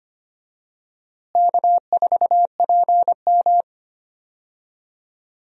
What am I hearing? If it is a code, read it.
Morse code, heard as a keyed tone, about 25 wpm, 700 Hz. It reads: K4PM
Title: USA Call Signs - 4 Characters - Repeated 3x - 25wpm